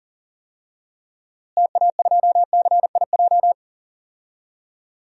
Morse code, heard as a keyed tone, about 40 wpm, 700 Hz. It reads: TA2CIJ
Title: DXCC-100 Call Signs - 6 Characters - Repeated 3x - 40wpm